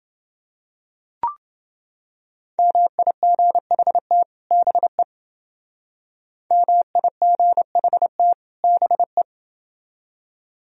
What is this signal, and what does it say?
Morse code, heard as a keyed tone, about 29 wpm, 700 Hz. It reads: MIGHT BE MIGHT BE